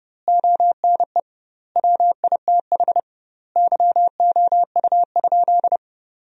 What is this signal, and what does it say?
Morse code, heard as a keyed tone, about 30 wpm, 700 Hz. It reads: ONE WITH YOU?
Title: 2x Word-Spacing - Sentences from Top 500 Words - 30wpm